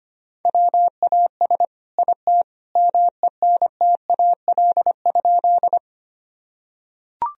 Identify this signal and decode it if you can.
Morse code, heard as a keyed tone, about 25 wpm, 700 Hz. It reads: WAS IT MENTAL?
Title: Sentences from Top 1000 Words - Review Part 4 - 25wpm